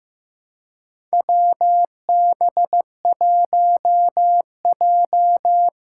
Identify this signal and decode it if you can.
Morse code, heard as a keyed tone, about 15 wpm, 700 Hz. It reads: WB1J
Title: USA Call Signs - 4 Characters - 3x Character Spacing - 15wpm